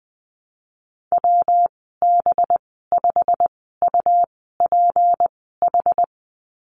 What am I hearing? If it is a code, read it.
Morse code, heard as a keyed tone, about 20 wpm, spaced as for 14 wpm, 700 Hz. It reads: WB5UPH
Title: 193 Lesson 31g - Letter v - Call Signs 20wpm